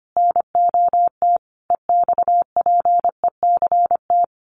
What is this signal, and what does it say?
Morse code, heard as a keyed tone, about 25 wpm, 700 Hz. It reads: NOT EXPECT
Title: Sentences from Top 1100 Words 25wpm